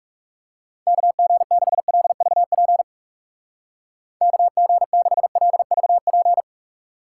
Morse code, heard as a keyed tone, about 39 wpm, 700 Hz. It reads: KG6LVP KG6LVP